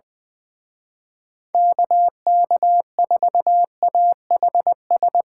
Morse code, heard as a keyed tone, about 22 wpm, 700 Hz. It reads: KK4AHS
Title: Speed Racing - USA Call Signs 20wpm